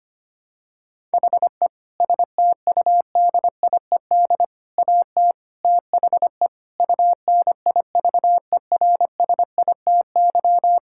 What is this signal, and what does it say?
Morse code, heard as a keyed tone, about 25 wpm, 700 Hz. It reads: HE STUDIED AT THE UNIVERSITY
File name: practice / 25wpm / Sentences with 4-Word Phrases - 25wpm